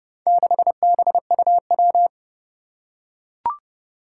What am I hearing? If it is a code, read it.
Morse code, heard as a keyed tone, about 30 wpm, 700 Hz. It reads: 6BUW
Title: Four Challenging Letter-Number Combinations 30wpm